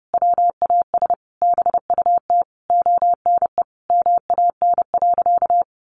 Morse code, heard as a keyed tone, about 28 wpm, 700 Hz. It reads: WAS BUT ONE MAN.
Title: Sentences from Top 1500 Words 30wpm